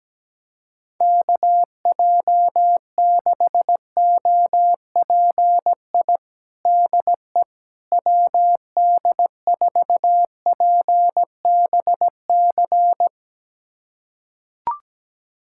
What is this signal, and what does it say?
Morse code, heard as a keyed tone, about 17 wpm, 700 Hz. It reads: KJ6OPI DE WD4PBC
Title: USA Callsign DE Callsign - 6 Characters - 17wpm